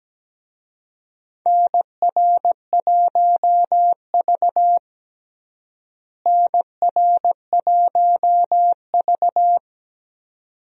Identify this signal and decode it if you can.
Morse code, heard as a keyed tone, about 17 wpm, 700 Hz. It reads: NR1V NR1V
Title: USA Call Signs - 4 Characters - Repeated 2x - 17wpm